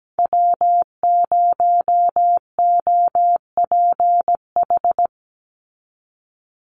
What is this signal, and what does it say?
Morse code, heard as a keyed tone, about 17 wpm, 700 Hz. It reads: W0OPH
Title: USA Call Signs 17wpm